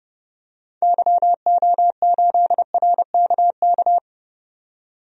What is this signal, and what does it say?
Morse code, heard as a keyed tone, about 30 wpm, 700 Hz. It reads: YO8RKK
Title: DXCC-100 Call Signs - 6 Characters - 30wpm